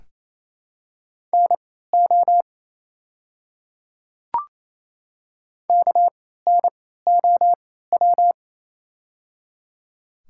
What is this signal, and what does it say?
Morse code, heard as a keyed tone, about 29 wpm, spaced as for 16 wpm, 690 Hz. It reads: NO KNOW
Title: 3x Character Spacing - Top 100 Words - 28wpm